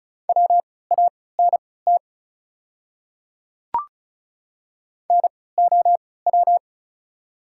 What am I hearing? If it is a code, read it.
Morse code, heard as a keyed tone, about 36 wpm, spaced as for 20 wpm, 690 Hz. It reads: WANT NOW
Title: 3x Character Spacing - Top 300 Words - 35wpm